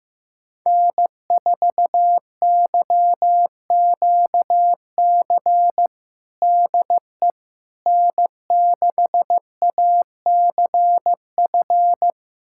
Morse code, heard as a keyed tone, about 15 wpm, 700 Hz. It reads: N4YQC DE N6ACF
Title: USA Callsign DE Callsign - 5 Characters - 15wpm